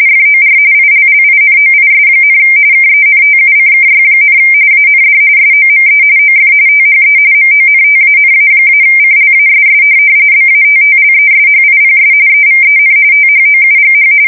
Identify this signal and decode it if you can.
RTTY, about 45.45 baud. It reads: FOR ILLINOIS, KENTUCKY, AND SOME OF THE MONTANA-CLASS BATTLESHIPS EVEN THOUGH
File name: fsk170 / USS-New-Jersey